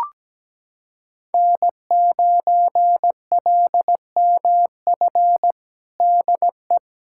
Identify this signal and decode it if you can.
Morse code, heard as a keyed tone, about 17 wpm, 700 Hz. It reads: N9LMF DE
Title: USA Callsign DE Callsign - 5 Characters - 17wpm